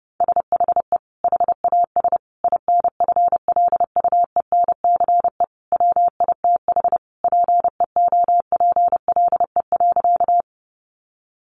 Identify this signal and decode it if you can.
Morse code, heard as a keyed tone, about 30 wpm, 700 Hz. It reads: SHE HAS INFLUENCE WITH PEOPLE.
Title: Sentences from Top 1100 Words 30wpm